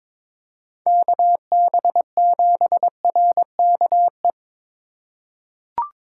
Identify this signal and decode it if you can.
Morse code, heard as a keyed tone, about 22 wpm, 700 Hz. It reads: KB7RKE
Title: USA Call Signs - 6 Characters - 4x Character Spacing - 22wpm